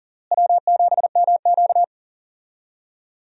Morse code, heard as a keyed tone, about 40 wpm, 700 Hz. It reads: W7MQ
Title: USA Call Signs - 4 Characters - Repeated 3x - 40wpm